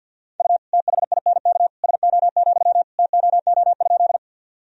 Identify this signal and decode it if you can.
Morse code, heard as a keyed tone, about 46 wpm, 700 Hz. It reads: U THINK SO, TOO?
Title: Sentences from Top 200 Words and Abbreviations 50wpm